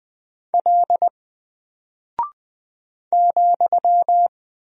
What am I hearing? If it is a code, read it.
Morse code, heard as a keyed tone, about 20 wpm, 700 Hz. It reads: L ,
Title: Single Character (Letter:Number:Punctuation:Prosign) 20wpm